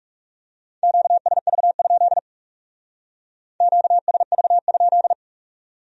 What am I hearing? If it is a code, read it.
Morse code, heard as a keyed tone, about 37 wpm, 700 Hz. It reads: QSV? QSV?
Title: Speed-Racing Q-Codes - 30wpm